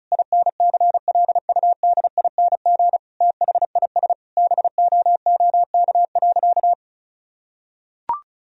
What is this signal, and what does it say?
Morse code, heard as a keyed tone, about 36 wpm, 700 Hz. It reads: INCLUDING THIS BOOK.
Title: Sentences from Top 1000 Words - Review Part 4 - 35wpm